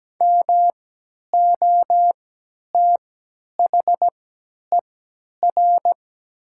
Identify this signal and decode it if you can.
Morse code, heard as a keyed tone, about 17 wpm, spaced as for 10 wpm, 700 Hz. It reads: MOTHER
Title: 3x Character Spacing - Top 200 Words - 17wpm